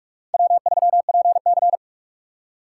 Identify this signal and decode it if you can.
Morse code, heard as a keyed tone, about 47 wpm, 700 Hz. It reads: W3PC
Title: USA Call Signs - 4 Characters - 2x Character Spacing - 45wpm